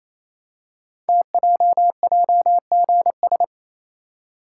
Morse code, heard as a keyed tone, about 28 wpm, 700 Hz. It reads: TJJGS
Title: Code Groups - 5 Letters - 28wpm